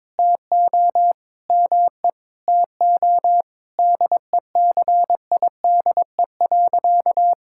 Morse code, heard as a keyed tone, about 21 wpm, 700 Hz. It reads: TO ME TO DECIDE.
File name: practice / 22wpm / Sentences with 2-Word Phrases - 22wpm